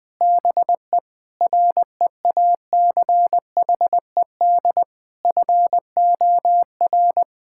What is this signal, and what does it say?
Morse code, heard as a keyed tone, about 20 wpm, 700 Hz. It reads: BE REACHED FOR
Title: Sentences with 7-Word Phrases - 20wpm